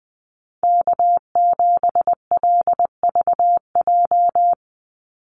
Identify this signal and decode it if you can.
Morse code, heard as a keyed tone, about 20 wpm, 700 Hz. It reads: K7LVJ
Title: Call Signs - 20wpm